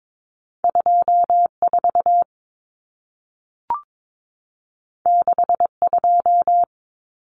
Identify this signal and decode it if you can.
Morse code, heard as a keyed tone, about 22 wpm, 700 Hz. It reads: 24 62
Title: Common Cut Numbers - 2-Digit 22wpm